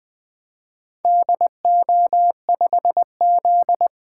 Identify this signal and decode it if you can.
Morse code, heard as a keyed tone, about 20 wpm, 700 Hz. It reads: DO5Z
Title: Four Letter-Number Combinations - 20wpm